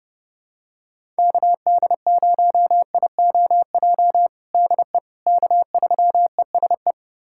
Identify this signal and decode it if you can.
Morse code, heard as a keyed tone, about 30 wpm, 700 Hz. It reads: KD0IOJ DE K3ESE
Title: USA Callsign DE Callsign 30wpm